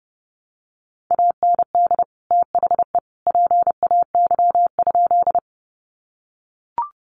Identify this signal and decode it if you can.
Morse code, heard as a keyed tone, about 30 wpm, 700 Hz. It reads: AND THE PAY?
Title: Sentences from Top 700 Words 30wpm